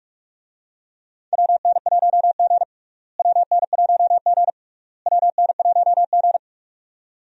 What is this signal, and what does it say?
Morse code, heard as a keyed tone, about 46 wpm, 700 Hz. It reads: WN1G WN1G WN1G
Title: USA Call Signs - 4 Characters - Repeated 3x - 45wpm